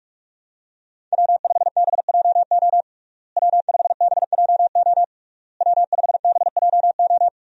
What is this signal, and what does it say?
Morse code, heard as a keyed tone, about 41 wpm, 700 Hz. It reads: W5BJO W5BJO W5BJO
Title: USA Call Signs - 5 Characters - Repeated 3x - 45wpm